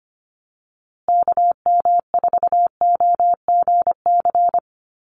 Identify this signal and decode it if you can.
Morse code, heard as a keyed tone, about 25 wpm, 700 Hz. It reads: KM4OGC
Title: USA Call Signs Repeated 3x 25wpm